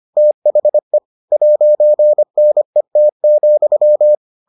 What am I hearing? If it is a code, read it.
Morse code, heard as a keyed tone, about 25 wpm, 600 Hz. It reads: THE 'NET,